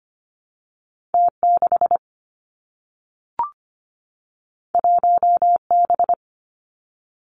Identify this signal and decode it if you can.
Morse code, heard as a keyed tone, about 25 wpm, 700 Hz. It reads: T 6 1 B
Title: Two Challenging Letter-Number Combinations 25wpm